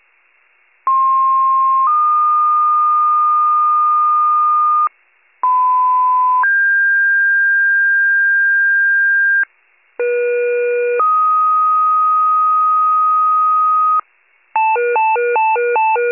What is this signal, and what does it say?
Two-tone page, heard as a one-second tone, then a three-second tone, about 1060 then 1220 Hz, about 1010 then 1600 Hz, about 480 then 1250 Hz. It is heard from 0.9 to 14.0 s.
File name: pager / 98 RESCUE / wcalert